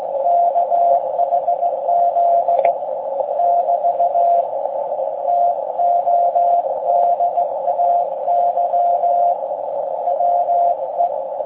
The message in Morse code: KHZ <BT> TODAY WE